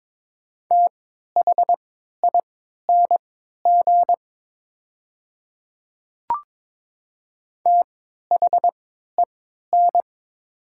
Morse code, heard as a keyed tone, about 22 wpm, spaced as for 12 wpm, 700 Hz. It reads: THING THEN
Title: 3x Character Spacing - Top 300 Words - 22wpm